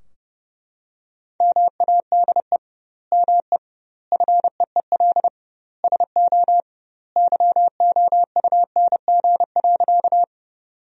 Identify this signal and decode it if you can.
Morse code, heard as a keyed tone, about 30 wpm, 700 Hz. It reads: MADE ME FEEL SO YOUNG.
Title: 2x Word-Spacing - Sentences from Top 400 Words - 30wpm